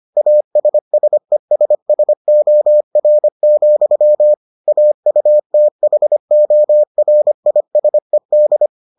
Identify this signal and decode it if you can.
Morse code, heard as a keyed tone, about 25 wpm, 600 Hz. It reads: ASSESSOR, AUTHORISED